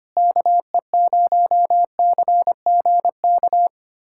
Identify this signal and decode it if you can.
Morse code, heard as a keyed tone, about 25 wpm, 700 Hz. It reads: KE0CGK
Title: USA Callsign DE Callsign 25wpm